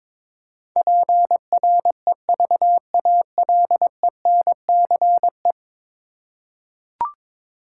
Morse code, heard as a keyed tone, about 21 wpm, 700 Hz. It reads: PREVALENCE
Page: morse-code-ninja-practice-22wpm